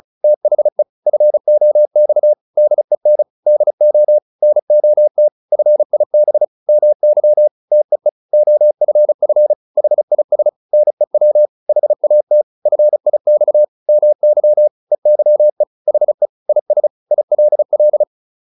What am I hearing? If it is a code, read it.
Morse code, heard as a keyed tone, about 34 wpm, 600 Hz. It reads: THE FOX DEN DO NOT FIB MY TEE OFF HIS NEW HAT FIX MY EYE HE IS ILL